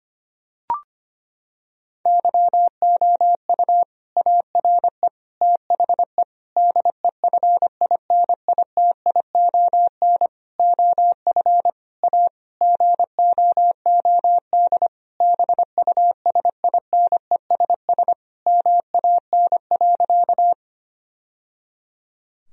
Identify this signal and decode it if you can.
Morse code, heard as a keyed tone, about 25 wpm, 700 Hz. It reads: YOU ARE THE DEFINITION OF A GOOD BUSINESS MAN.
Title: Sentences from Top 1000 Words - Review Part 4 - 25wpm